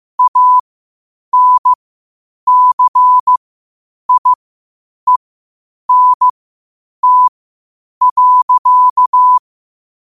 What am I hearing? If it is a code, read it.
Morse code, heard as a keyed tone, about 15 wpm, spaced as for 8 wpm, 1000 Hz. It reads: ANCIENT.
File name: qotdfiles / 8wpm